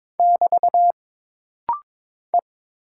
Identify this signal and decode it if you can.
Morse code, heard as a keyed tone, about 22 wpm, 700 Hz. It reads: <BT> E
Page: This practice set is composed of Letters, Numbers, common punctuations, and common prosigns.